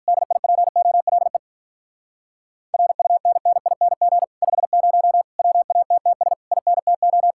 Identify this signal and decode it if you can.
Morse code, heard as a keyed tone, about 46 wpm, 700 Hz. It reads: DIPOLE RUNNING 50 WATTS INTO